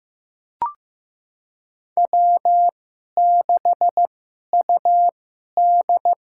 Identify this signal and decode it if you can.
Morse code, heard as a keyed tone, about 15 wpm, spaced as for 11 wpm, 700 Hz. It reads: W6UD
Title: USA Call Signs - 4 Characters - 2x Character Spacing - 15wpm